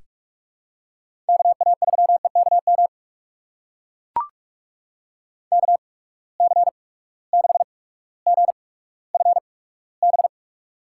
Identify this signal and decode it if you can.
Morse code, heard as a keyed tone, about 44 wpm, spaced as for 19 wpm, 700 Hz. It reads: KA3EKM KC6GFB
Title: USA Call Signs - 6 Characters - 8x Character Spacing - 45wpm